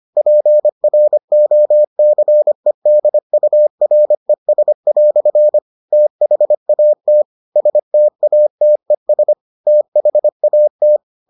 Morse code, heard as a keyed tone, about 25 wpm, 600 Hz. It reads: PROCEDURES" THAT STATES THAT